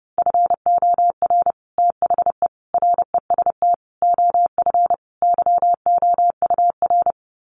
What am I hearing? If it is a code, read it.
Morse code, heard as a keyed tone, about 30 wpm, 700 Hz. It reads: FOR THE REST OF YOUR
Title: Sets of 5 Words Encore 30wpm